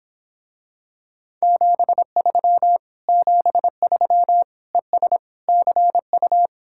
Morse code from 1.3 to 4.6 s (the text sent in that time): 73 73